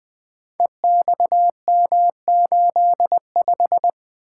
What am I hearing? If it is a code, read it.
Morse code, heard as a keyed tone, about 20 wpm, 700 Hz. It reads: EXM85
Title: Code Groups - 5 Letter:Numbers - 20wpm